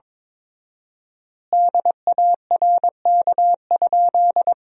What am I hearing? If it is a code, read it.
Morse code, heard as a keyed tone, about 22 wpm, 700 Hz. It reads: DARK?
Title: 4x Word-Spacing - Sentences from Top 500 Words - 22wpm